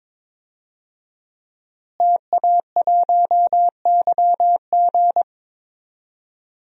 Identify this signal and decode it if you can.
Morse code, heard as a keyed tone, about 22 wpm, 700 Hz. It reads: TA1YG
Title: DXCC-100 Call Signs - 5 Characters - Repeated 3x - 22wpm